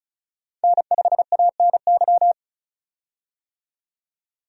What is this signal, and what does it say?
Morse code, heard as a keyed tone, about 36 wpm, 700 Hz. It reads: N5ANY
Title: USA Call Signs - 5 Characters - 35wpm